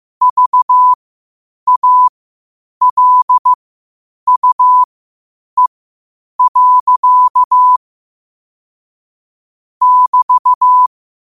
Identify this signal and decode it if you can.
Morse code, heard as a keyed tone, about 15 wpm, spaced as for 8 wpm, 1000 Hz. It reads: VALUE. <BT>
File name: qotdfiles / 8wpm